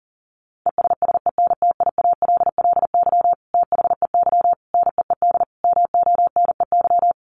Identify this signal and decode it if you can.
Morse code, heard as a keyed tone, about 40 wpm, 700 Hz. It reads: ESSENTIALLY THEY NEED MONEY
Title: Sentences from Top 1300 Words 40wpm